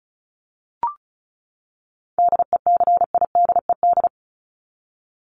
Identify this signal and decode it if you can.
Morse code, heard as a keyed tone, about 34 wpm, 700 Hz. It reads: DECIDED
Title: ED Suffix - Top 100 Words with Base Words 35wpm